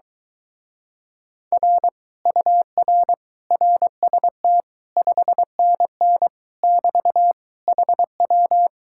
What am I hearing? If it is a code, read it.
Morse code, heard as a keyed tone, about 24 wpm, 700 Hz. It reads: R UR RST 5NN <BT> HW